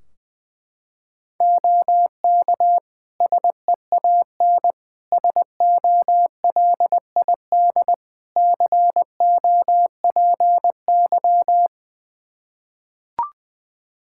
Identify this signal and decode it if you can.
Morse code, heard as a keyed tone, about 21 wpm, 700 Hz. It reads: OK SEAN SOLID COPY
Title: Practice QSO - Exchange 3 - More Info - 20wpm